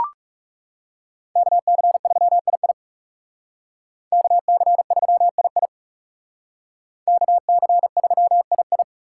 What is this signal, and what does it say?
Morse code, heard as a keyed tone, about 36 wpm, 700 Hz. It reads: KC3II KC3II KC3II
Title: Speed Racing - USA 2x2 Call Signs 30wpm